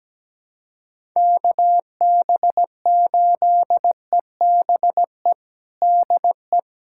Morse code, heard as a keyed tone, about 17 wpm, 700 Hz. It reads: KB8EBE DE
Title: USA Callsign DE Callsign - 6 Characters - 17wpm